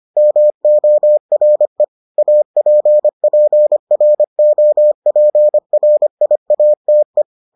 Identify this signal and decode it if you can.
Morse code, heard as a keyed tone, about 25 wpm, 600 Hz. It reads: MORE APPROPRIATE